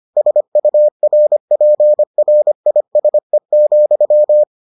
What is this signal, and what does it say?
Morse code, heard as a keyed tone, about 25 wpm, 600 Hz. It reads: SURPRISE,